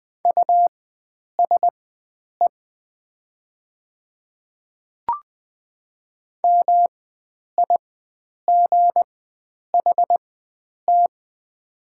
Morse code, heard as a keyed tone, about 20 wpm, spaced as for 9 wpm, 700 Hz. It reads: USE MIGHT
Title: 4x Character Spacing - Top 300 Words - 20wpm